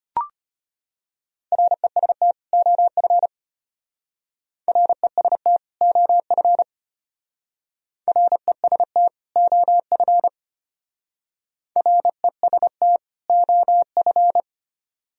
Morse code, heard as a keyed tone, about 32 wpm, 700 Hz. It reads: REST OF REST OF REST OF REST OF